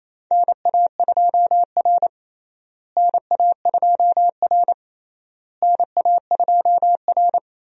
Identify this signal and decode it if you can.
Morse code, heard as a keyed tone, about 28 wpm, 700 Hz. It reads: NA2R NA2R NA2R
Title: USA Call Signs - 4 Characters - Repeated 3x - 28wpm